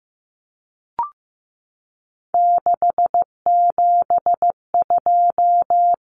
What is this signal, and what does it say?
Morse code, heard as a keyed tone, about 15 wpm, 700 Hz. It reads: 672
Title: Numbers - 3-Digit 15wpm